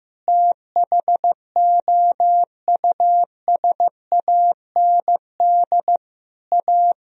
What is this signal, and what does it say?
Morse code, heard as a keyed tone, about 16 wpm, 700 Hz. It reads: THOUSAND A
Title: Sentences from Top 1000 Words - Review Part 3 - 15wpm